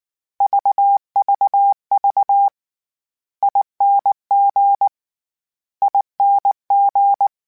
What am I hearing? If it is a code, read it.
Morse code, heard as a keyed tone, about 19 wpm, 800 Hz. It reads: VVV ING ING
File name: Special INGs / Special INGs-mp3s / SlowToFaster ing @18